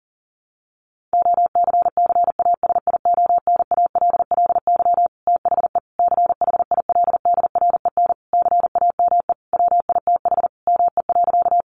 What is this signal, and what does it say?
Morse code, heard as a keyed tone, about 40 wpm, 700 Hz. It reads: OCCASIONALLY THE CHILDREN CAME WITH ME.
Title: Sentences from Top 1300 Words 40wpm